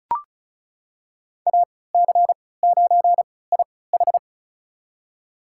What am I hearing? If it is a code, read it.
Morse code, heard as a keyed tone, about 36 wpm, spaced as for 20 wpm, 700 Hz. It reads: AC9IH
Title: USA Call Signs - 5 Characters - 3x Character Spacing - 35wpm